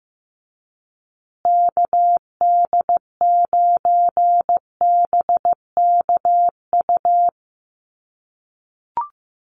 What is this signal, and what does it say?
Morse code, heard as a keyed tone, about 15 wpm, 700 Hz. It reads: KD9BKU
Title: USA Call Signs Repeated 2x 15wpm